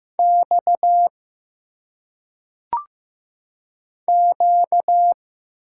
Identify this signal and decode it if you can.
Morse code, heard as a keyed tone, about 15 wpm, 700 Hz. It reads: X Q
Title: Single Letters - Phonetic Pronunciation 15wpm